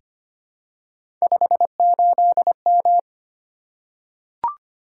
Morse code, heard as a keyed tone, about 25 wpm, 700 Hz. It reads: 58M
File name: Three Letter-Number Combinations - 25wpm